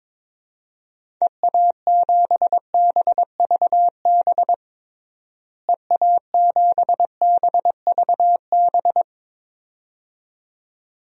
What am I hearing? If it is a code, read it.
Morse code, heard as a keyed tone, about 22 wpm, 700 Hz. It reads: EA7BVB EA7BVB
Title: DXCC-100 Call Signs - 6 Characters - Repeated 2x - 22wpm